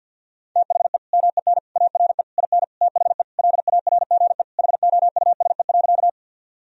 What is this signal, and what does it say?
Morse code, heard as a keyed tone, about 47 wpm, 700 Hz. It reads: THE MEN ARE IN THE LARGE HOUSE.
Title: Sentences from Top 1000 Words - Review Part 1 - 50wpm